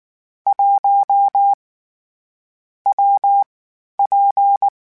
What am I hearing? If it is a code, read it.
Morse code, heard as a keyed tone, about 19 wpm, 800 Hz. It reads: 1 W P
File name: Variable Speed / Variable Speed-mp3s / WPJ WPJ1 sp @18